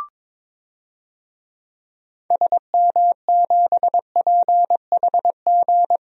Morse code, heard as a keyed tone, about 22 wpm, 700 Hz. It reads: SM7PHG
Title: DXCC-100 Call Signs - 6 Characters - 22wpm